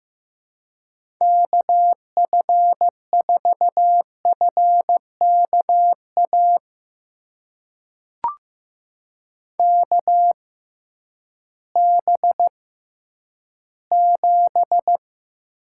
Morse code, heard as a keyed tone, about 15 wpm, spaced as for 10 wpm, 700 Hz. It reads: KF4FKA KB7
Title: USA Call Signs - 6 Characters - 6x Character Spacing - 15wpm